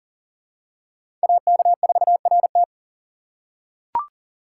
Morse code, heard as a keyed tone, about 40 wpm, 690 Hz. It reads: AK4RT
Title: USA Call Signs - 5 Characters - 3x Character Spacing - 40wpm